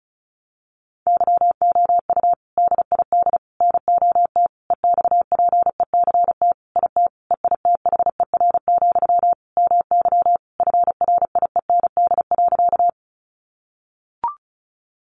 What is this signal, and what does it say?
Morse code, heard as a keyed tone, about 34 wpm, 700 Hz. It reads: YOU DID NOT EXPECT IT EITHER, MY FRIEND.
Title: Sentences from Top 1100 Words 35wpm